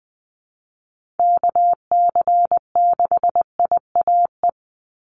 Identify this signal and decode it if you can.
Morse code, heard as a keyed tone, about 19 wpm, 700 Hz. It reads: KC6IAE
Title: USA Call Signs 20wpm